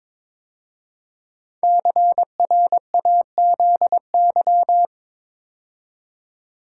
Morse code, heard as a keyed tone, about 22 wpm, 700 Hz. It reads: CRAZY
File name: 5 Letter Words - First 3 Letters Given - 22wpm